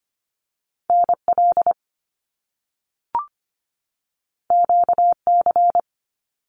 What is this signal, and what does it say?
Morse code, heard as a keyed tone, about 25 wpm, 700 Hz. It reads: NL QC
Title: Canadian Provinces and Territories 25wpm